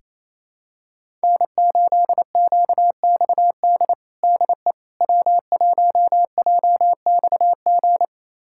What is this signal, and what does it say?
Morse code, heard as a keyed tone, about 28 wpm, 700 Hz. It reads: N8QXD DE W1JXG
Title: USA Callsign DE Callsign 28wpm